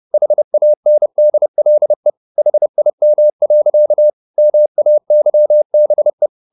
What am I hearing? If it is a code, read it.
Morse code, heard as a keyed tone, about 30 wpm, 600 Hz. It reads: HANDLE HIM. MAYBE
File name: the old man and the sea / the old man and the sea 1 30wpm